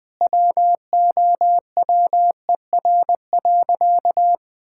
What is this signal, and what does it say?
Morse code, heard as a keyed tone, about 20 wpm, 700 Hz. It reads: WOWER.
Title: Pangram Challenge v2 - 20wpm